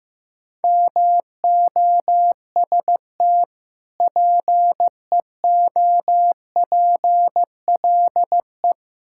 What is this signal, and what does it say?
Morse code, heard as a keyed tone, about 15 wpm, 700 Hz. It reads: MOST PEOPLE
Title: Sentences from Top 1000 Words - Review Part 6 - 15wpm